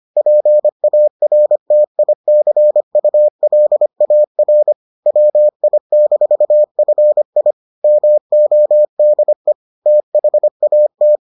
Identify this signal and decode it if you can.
Morse code, heard as a keyed tone, about 25 wpm, 600 Hz. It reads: PARTICULAR WI-FI MODE THAT